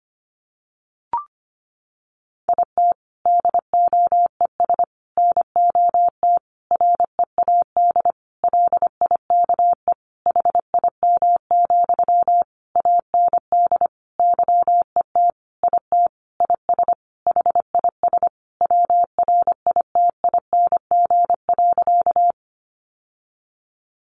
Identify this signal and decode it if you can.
Morse code, heard as a keyed tone, about 25 wpm, 700 Hz. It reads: IT DOES NOT READ LIKE HIM, AND YET IT IS HIS WRITING.
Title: Sentences from Top 800 Words 25wpm